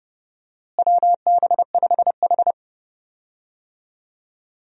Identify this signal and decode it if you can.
Morse code, heard as a keyed tone, about 30 wpm, 700 Hz. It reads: WB5H
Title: USA Call Signs - 4 Characters - Repeated 3x - 30wpm